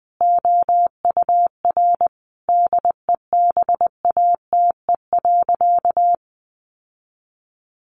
Morse code, heard as a keyed tone, about 19 wpm, 700 Hz. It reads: OUR DEBATE.
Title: Sentences from Top 1100 Words 20wpm